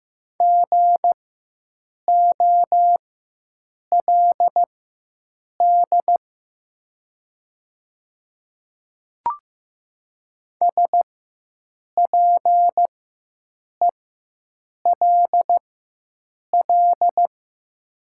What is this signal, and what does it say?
Morse code, heard as a keyed tone, about 15 wpm, spaced as for 7 wpm, 700 Hz. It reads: GOLD SPELL
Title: 4x Character Spacing - Top 500 Words - 15wpm